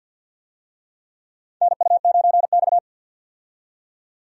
Morse code, heard as a keyed tone, about 51 wpm, 700 Hz. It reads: NU9X
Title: USA Call Signs - 4 Characters - 50wpm